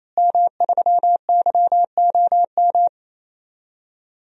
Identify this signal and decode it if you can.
Morse code, heard as a keyed tone, about 28 wpm, 700 Hz. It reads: M3YOM
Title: DXCC-100 Call Signs - 5 Characters - Repeated 3x - 28wpm